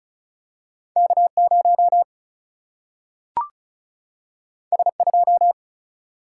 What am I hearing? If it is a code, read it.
Morse code, heard as a keyed tone, about 36 wpm, 700 Hz. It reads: K0 S2
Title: Single-Letter Single-Number 35wpm